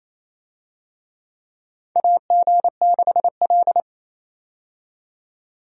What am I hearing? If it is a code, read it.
Morse code, heard as a keyed tone, about 27 wpm, 700 Hz. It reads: AG6L
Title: USA Call Signs - 4 Characters - 28wpm